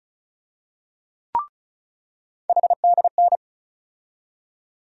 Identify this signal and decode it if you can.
Morse code, heard as a keyed tone, about 34 wpm, 700 Hz. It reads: HDN
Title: USA Call Sign Postfixes - 3 Letters 35wpm